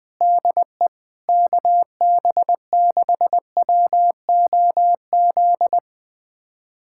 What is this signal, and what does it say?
Morse code, heard as a keyed tone, about 21 wpm, 700 Hz. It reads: DE KB6WOZ
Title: USA Callsign DE Callsign 20wpm